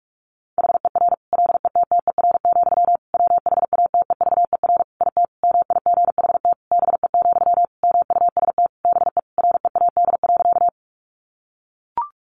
Morse code, heard as a keyed tone, about 44 wpm, 700 Hz. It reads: HER LETTER, WHATEVER IT MIGHT BE, MUST BE READ.
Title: Sentences from Top 1100 Words 45wpm